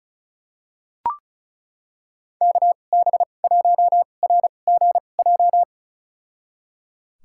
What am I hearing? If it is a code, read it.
Morse code, heard as a keyed tone, about 36 wpm, spaced as for 25 wpm, 700 Hz. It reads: KB1RGJ
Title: USA Call Signs - 6 Characters - 2x Character Spacing - 35wpm